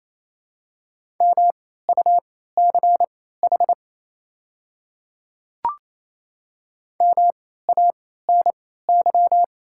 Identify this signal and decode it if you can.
Morse code, heard as a keyed tone, about 28 wpm, spaced as for 16 wpm, 690 Hz. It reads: MUCH MANY
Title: 3x Character Spacing - Top 500 Words - 28wpm